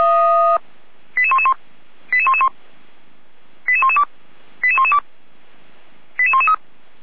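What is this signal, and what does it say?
Five-tone selective call (ZVEI), heard as quick runs of five tones, tones of about 80 ms. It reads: 88111 88111 88112 88112 88113
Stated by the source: ZVEI five-tone call 88497 received on an SDR, logged by feuerwehr-orlamuende.org fire brigade.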